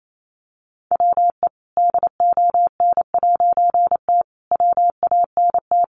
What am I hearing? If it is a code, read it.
Morse code, heard as a keyed tone, about 28 wpm, 700 Hz. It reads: WE DON'T WANT
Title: A Sound Of Thunder 28wpm